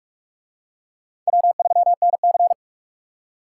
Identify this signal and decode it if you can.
Morse code, heard as a keyed tone, about 39 wpm, 700 Hz. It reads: W3NC